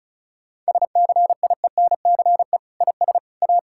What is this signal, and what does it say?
Morse code, heard as a keyed tone, about 36 wpm, 700 Hz. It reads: SCIENCE IS A
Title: Sentences from Top 1000 Words - Review Part 6 - 35wpm